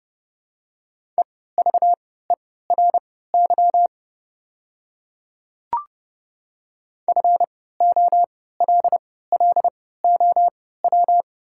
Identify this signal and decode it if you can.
Morse code, heard as a keyed tone, about 30 wpm, spaced as for 17 wpm, 700 Hz. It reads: EVERY FOLLOW
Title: 3x Character Spacing - Top 200 Words - 30wpm